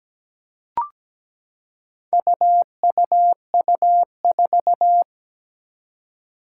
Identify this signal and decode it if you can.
Morse code, heard as a keyed tone, about 17 wpm, 700 Hz. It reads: UUU4
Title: Four Challenging Letter-Number Combinations 17wpm